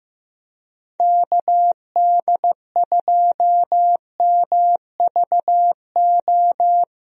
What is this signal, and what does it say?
Morse code, heard as a keyed tone, about 16 wpm, 700 Hz. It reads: KD2MVO
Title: USA Call Signs - 6 Characters - Repeated 2x - 15wpm